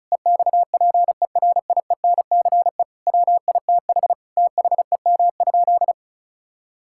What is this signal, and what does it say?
Morse code, heard as a keyed tone, about 34 wpm, 700 Hz. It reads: EXPERIENCE WITH THEM?
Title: Sentences from Top 1000 Words - Review Part 5 - 35wpm